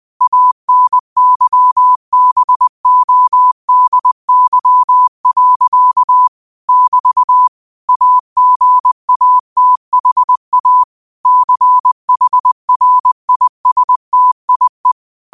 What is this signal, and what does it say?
Morse code, heard as a keyed tone, about 20 wpm, 1000 Hz. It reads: ANYBODY. <BT> AGATHA CHRISTIE